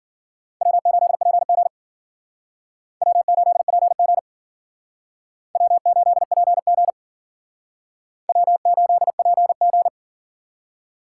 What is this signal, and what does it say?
Morse code, heard as a keyed tone, about 52 wpm, 700 Hz. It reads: W8PG W8PG W8PG W8PG